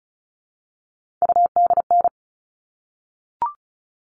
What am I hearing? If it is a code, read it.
Morse code, heard as a keyed tone, about 34 wpm, 700 Hz. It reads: UDN
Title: All Cut Numbers - 3-Digit 35wpm